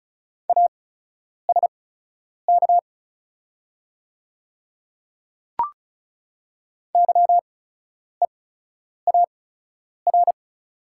Morse code, heard as a keyed tone, about 36 wpm, spaced as for 10 wpm, 700 Hz. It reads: ASK YEAR
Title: 8x Character Spacing - Top 300 Words - 35wpm